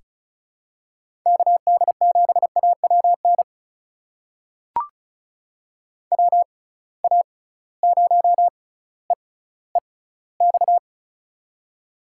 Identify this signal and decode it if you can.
Morse code, heard as a keyed tone, about 36 wpm, spaced as for 18 wpm, 700 Hz. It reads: KD7AWN WA0EEX
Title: USA Call Signs - 6 Characters - 6x Character Spacing - 35wpm